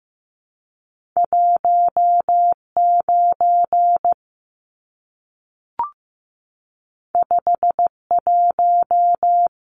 Morse code, heard as a keyed tone, about 15 wpm, 700 Hz. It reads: 19 51
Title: Numbers - 2-Digit 15wpm